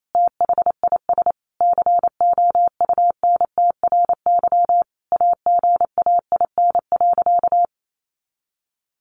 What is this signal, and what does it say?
Morse code, heard as a keyed tone, about 28 wpm, 700 Hz. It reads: THIS COUNTRY AGAIN.
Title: Sentences from Top 1200 Words 28wpm